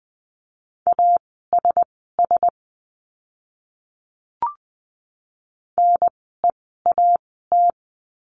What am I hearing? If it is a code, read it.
Morse code, heard as a keyed tone, about 20 wpm, spaced as for 14 wpm, 700 Hz. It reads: ASS NEAT
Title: 036 Lesson 7f - Letter s - Words 20wpm